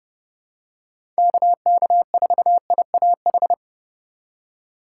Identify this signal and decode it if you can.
Morse code, heard as a keyed tone, about 32 wpm, 700 Hz. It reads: KK4IAH
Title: USA Call Signs - 3x Character Spacing - 30wpm